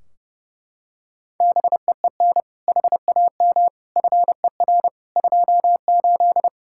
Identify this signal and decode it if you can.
Morse code, heard as a keyed tone, about 32 wpm, 700 Hz. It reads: BEEN HAM FER 28
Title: K1USN SST QSO Practice 30wpm